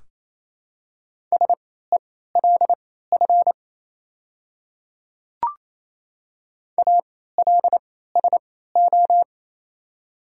Morse code, heard as a keyed tone, about 27 wpm, spaced as for 16 wpm, 690 Hz. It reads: SELF ALSO
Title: 3x Character Spacing - Top 200 Words - 28wpm